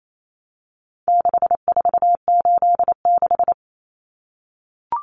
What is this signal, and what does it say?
Morse code, heard as a keyed tone, about 28 wpm, 700 Hz. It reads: 6486
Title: Numbers - 4-Digit 28wpm